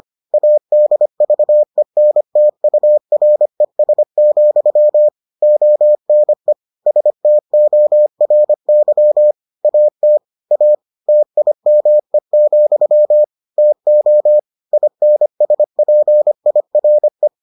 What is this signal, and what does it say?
Morse code, heard as a keyed tone, about 25 wpm, 600 Hz. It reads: ADVENTURES, ONE STORY AT A TIME, TO INSPIRE